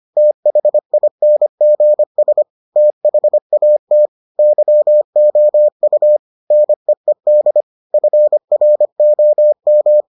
Morse code, heard as a keyed tone, about 25 wpm, 600 Hz. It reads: THINGS THAT YOU NEED FROM